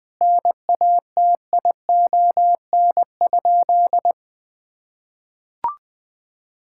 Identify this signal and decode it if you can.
Morse code, heard as a keyed tone, about 20 wpm, 700 Hz. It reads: NATION?
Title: Sentences from Top 1000 Words - Review Part 6 - 20wpm